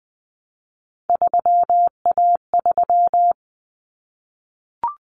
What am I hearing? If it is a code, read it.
Morse code, heard as a keyed tone, about 20 wpm, 700 Hz. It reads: 3A3
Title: Common Cut Numbers - 3-Digit 20wpm